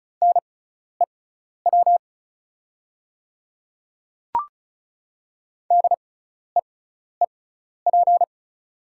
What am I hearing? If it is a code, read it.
Morse code, heard as a keyed tone, about 36 wpm, spaced as for 12 wpm, 700 Hz. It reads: NEW DEEP
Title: 6x Character Spacing - Top 500 Words - 35wpm